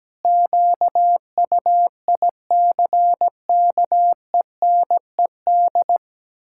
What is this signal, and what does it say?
Morse code, heard as a keyed tone, about 17 wpm, 700 Hz. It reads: QUICKENED
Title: Pangram Challenge v2 - 17wpm